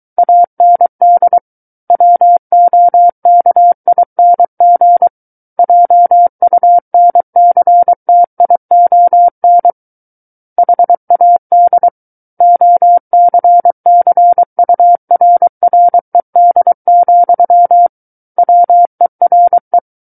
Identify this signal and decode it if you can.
Morse code, heard as a keyed tone, about 23 wpm, 700 Hz. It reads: AND WOKING JUNCTION HAD OCCURRED, WERE
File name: Book One / 14-Chapter 14 - 23 WPM